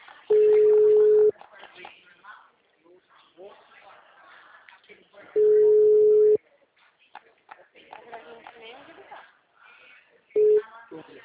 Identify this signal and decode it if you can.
Ringback tone (Europe, 425 Hz), with speech.